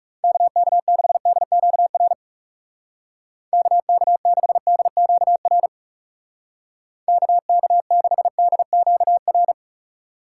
Morse code, heard as a keyed tone, about 34 wpm, 700 Hz. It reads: KK6DQR KK6DQR KK6DQR
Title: Speed Racing - USA 2x3 Call Signs 30wpm